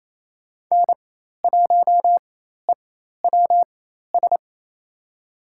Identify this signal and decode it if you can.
Morse code, heard as a keyed tone, about 28 wpm, spaced as for 13 wpm, 700 Hz. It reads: N1EWS
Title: USA Call Signs - 5 Characters - 4x Character Spacing - 28wpm